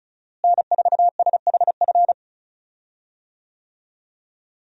Morse code, heard as a keyed tone, about 36 wpm, 700 Hz. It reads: N4SHF
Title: USA Call Signs - 5 Characters - Repeated 2x - 35wpm